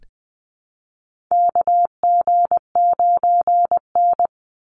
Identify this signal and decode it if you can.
Morse code, heard as a keyed tone, about 20 wpm, 700 Hz. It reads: KG9N
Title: Call Signs - 20wpm